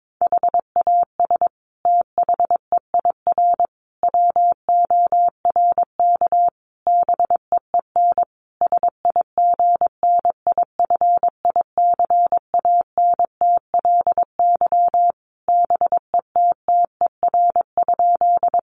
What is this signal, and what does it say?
Morse code, heard as a keyed tone, about 22 wpm, 700 Hz. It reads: HAS THEIR WORK BEEN SIGNIFICANTLY BETTER?
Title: Sentences from Top 1500 Words 22wpm